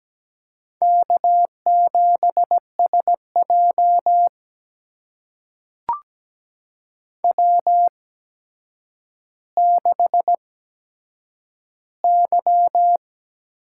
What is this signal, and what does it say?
Morse code, heard as a keyed tone, about 17 wpm, spaced as for 8 wpm, 700 Hz. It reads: K7SJ W6Y
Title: USA Call Signs - 8x Character Spacing - 17wpm